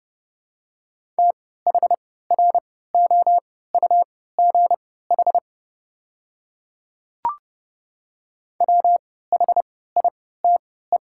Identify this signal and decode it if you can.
Morse code, heard as a keyed tone, about 30 wpm, spaced as for 17 wpm, 700 Hz. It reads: THROUGH WHITE
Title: 3x Character Spacing - Top 300 Words - 30wpm